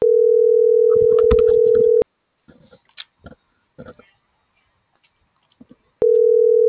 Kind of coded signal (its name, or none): telephone call progress tone